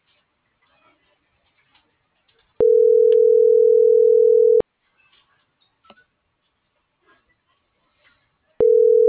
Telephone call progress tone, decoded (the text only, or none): ringback tone